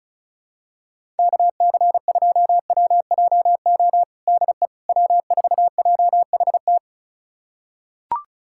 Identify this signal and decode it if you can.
Morse code, heard as a keyed tone, about 34 wpm, 700 Hz. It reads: KC2WJO DE W4JHT
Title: USA Callsign DE Callsign 35wpm